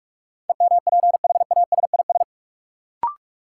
Morse code, heard as a keyed tone, about 46 wpm, 700 Hz. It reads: EMPHASIS
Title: Sentences from Top 1400 Words - New Word Drill Alt - 45wpm